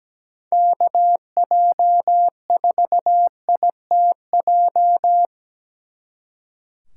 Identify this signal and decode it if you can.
Morse code, heard as a keyed tone, about 20 wpm, 700 Hz. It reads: KJ4ITJ
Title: Speed Racing - USA 2x3 Call Signs 17wpm